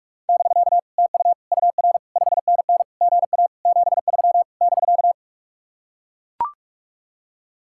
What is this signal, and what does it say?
Morse code, heard as a keyed tone, about 44 wpm, 700 Hz. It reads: <BK> TU UR 5NN GA 73 <BK>